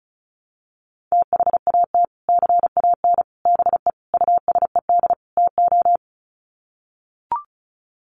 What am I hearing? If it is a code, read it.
Morse code, heard as a keyed tone, about 34 wpm, 700 Hz. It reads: THAT CAN BE USED TO
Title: Sets of 5 Words Encore 35wpm